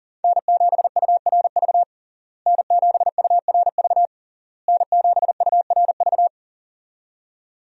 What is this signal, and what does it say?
Morse code, heard as a keyed tone, about 40 wpm, 700 Hz. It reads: N7URV N7URV N7URV
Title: USA Call Signs - 5 Characters - Repeated 3x - 40wpm